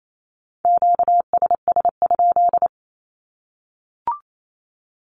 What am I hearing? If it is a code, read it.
Morse code, heard as a keyed tone, about 28 wpm, 700 Hz. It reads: QSS?
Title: Q-Codes Repeated 3x 28wpm